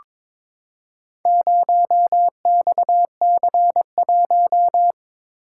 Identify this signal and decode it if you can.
Morse code, heard as a keyed tone, about 23 wpm, 700 Hz. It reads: 0XC1
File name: practice / 22wpm / Four Letter-Number Combinations - 22wpm